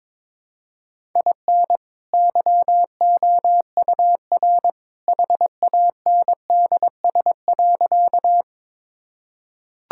Morse code, heard as a keyed tone, about 21 wpm, 700 Hz. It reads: IN YOUR HANDS.
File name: practice / 22wpm / Sentences with 2-Word Phrases - 22wpm